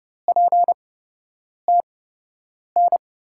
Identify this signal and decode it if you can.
Morse code, heard as a keyed tone, about 30 wpm, spaced as for 8 wpm, 700 Hz. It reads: PTN
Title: USA Call Signs - 6 Characters - 8x Character Spacing - 30wpm